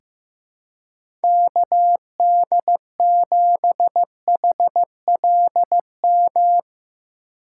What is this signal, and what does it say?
Morse code, heard as a keyed tone, about 16 wpm, 700 Hz. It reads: KD7HLM
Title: USA Call Signs - 6 Characters - 4x Character Spacing - 15wpm